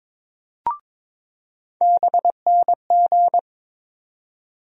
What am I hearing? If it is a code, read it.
Morse code, heard as a keyed tone, about 22 wpm, 700 Hz. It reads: BNG
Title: USA Call Sign Postfixes - 3 Letters 22wpm